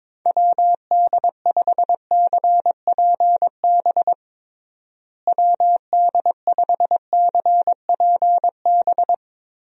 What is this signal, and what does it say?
Morse code, heard as a keyed tone, about 22 wpm, 700 Hz. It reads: WD5CPB WD5CPB
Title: USA Call Signs - 6 Characters - Repeated 2x - 22wpm